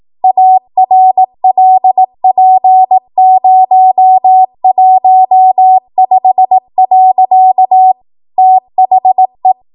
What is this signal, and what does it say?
Morse code, heard as a keyed tone, about 18 wpm, 750 Hz. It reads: ARLP015. THE
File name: Morse / 18WPM